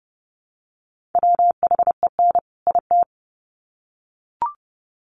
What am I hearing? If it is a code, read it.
Morse code, heard as a keyed tone, about 30 wpm, 700 Hz. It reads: WHEN IT
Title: Sets of 2 Words 30wpm